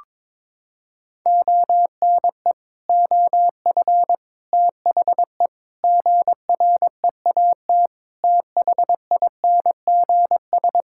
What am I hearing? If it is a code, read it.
Morse code, heard as a keyed tone, about 22 wpm, 700 Hz. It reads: ONE OF THE GREAT THINGS
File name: practice / 22wpm / Sets of 6 Words - 22wpm